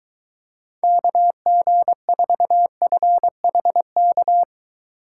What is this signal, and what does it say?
Morse code, heard as a keyed tone, about 22 wpm, 700 Hz. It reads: KG4FHK